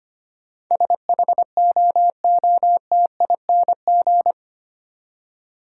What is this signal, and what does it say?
Morse code, heard as a keyed tone, about 25 wpm, 700 Hz. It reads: SHOOTING
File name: Eight Letter Words - 25wpm